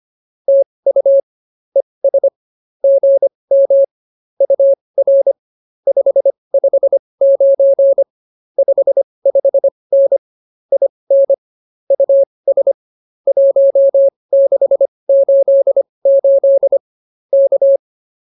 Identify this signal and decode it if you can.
Morse code, heard as a keyed tone, about 24 wpm, spaced as for 20 wpm, 550 Hz. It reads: TU ES GM UR 559 55N IN US 1688 K